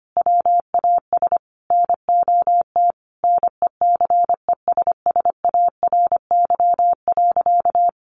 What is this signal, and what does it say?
Morse code, heard as a keyed tone, about 25 wpm, 700 Hz. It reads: WAS NOT NECESSARY.
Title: Sentences from Top 800 Words 25wpm